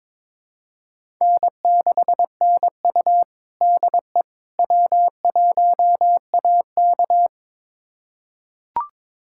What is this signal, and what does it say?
Morse code, heard as a keyed tone, about 22 wpm, 700 Hz. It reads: N6NU DE W1AK
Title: USA Callsign DE Callsign - 4 Characters - 22wpm